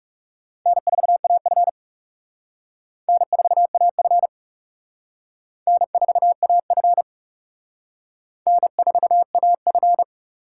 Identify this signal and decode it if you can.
Morse code, heard as a keyed tone, about 35 wpm, 700 Hz. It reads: N4AF N4AF N4AF N4AF
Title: Speed Racing - USA 1x2 Call Signs 30wpm